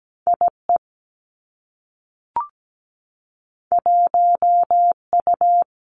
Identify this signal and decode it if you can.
Morse code, heard as a keyed tone, about 17 wpm, 700 Hz. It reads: I E 1 U
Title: Two Challenging Letter-Number Combinations 17wpm